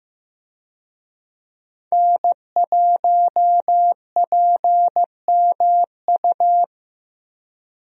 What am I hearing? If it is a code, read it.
Morse code, heard as a keyed tone, about 15 wpm, 700 Hz. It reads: N1PMU
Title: USA Call Signs - 5 Characters - 15wpm